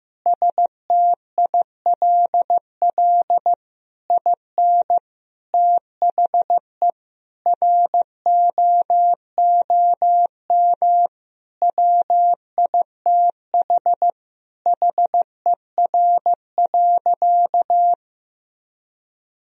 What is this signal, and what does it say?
Morse code, heard as a keyed tone, about 15 wpm, 700 Hz. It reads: STILL IN THE ROOM WITH HER.
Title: Sentences from Top 1000 Words - Review Part 2 - 15wpm